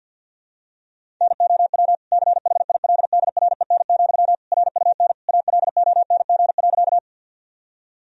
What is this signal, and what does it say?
Morse code, heard as a keyed tone, about 47 wpm, 700 Hz. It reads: NOW CHILDREN, RUN ALONG.
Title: Sentences from Top 1000 Words - Review Part 1 - 50wpm